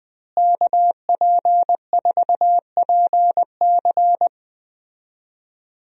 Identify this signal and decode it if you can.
Morse code, heard as a keyed tone, about 20 wpm, 700 Hz. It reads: KP4PC
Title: DXCC-100 Call Signs - 5 Characters - Repeated 3x - 20wpm